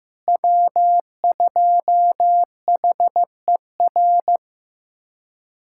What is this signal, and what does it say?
Morse code, heard as a keyed tone, about 16 wpm, 700 Hz. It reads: W2HER
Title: USA Callsign DE Callsign - 5 Characters - 15wpm